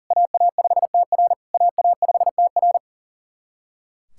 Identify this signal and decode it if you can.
Morse code, heard as a keyed tone, about 38 wpm, 690 Hz. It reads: AA5TR AA5TR
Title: SOTA QSO Practice 40wpm